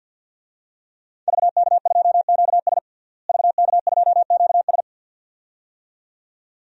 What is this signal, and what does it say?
Morse code, heard as a keyed tone, about 51 wpm, 700 Hz. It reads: VK2QS VK2QS
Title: DXCC-100 Call Signs - 5 Characters - Repeated 2x - 50wpm